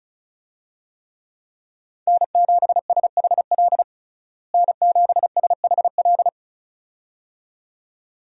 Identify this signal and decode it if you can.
Morse code, heard as a keyed tone, about 36 wpm, 700 Hz. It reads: N7SHL N7SHL
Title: USA Call Signs - 5 Characters - Repeated 2x - 35wpm